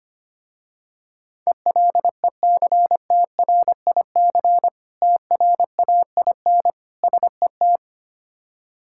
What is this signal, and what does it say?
Morse code, heard as a keyed tone, about 25 wpm, 700 Hz. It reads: ELECTRIC TRAIN SET
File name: practice / 25wpm / Difficult Words with Hints - 25wpm